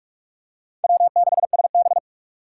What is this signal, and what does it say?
Morse code, heard as a keyed tone, about 46 wpm, 700 Hz. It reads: W6SB
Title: USA Call Signs - 4 Characters - 4x Character Spacing - 45wpm